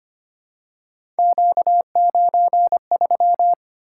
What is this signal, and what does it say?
Morse code, heard as a keyed tone, about 25 wpm, 700 Hz. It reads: Q93
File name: Three Letter-Number Combinations - 25wpm